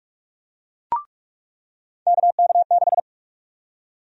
Morse code, heard as a keyed tone, about 46 wpm, 700 Hz. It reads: KK6
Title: USA Call Sign Prefixes - 45wpm